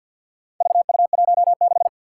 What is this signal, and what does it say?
Morse code, heard as a keyed tone, about 53 wpm, 700 Hz. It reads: VU16
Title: Four Challenging Letter-Number Combinations 50wpm